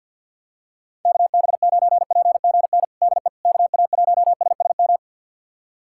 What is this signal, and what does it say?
Morse code, heard as a keyed tone, about 50 wpm, 700 Hz. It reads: KB9PGN DE KA1SSM
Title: USA Callsign DE Callsign - 6 Characters - 50wpm